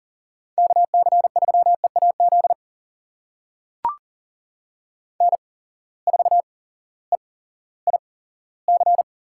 Morse code, heard as a keyed tone, about 40 wpm, spaced as for 18 wpm, 700 Hz. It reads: KC3EAZ N4EIC
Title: USA Call Signs - 8x Character Spacing - 40wpm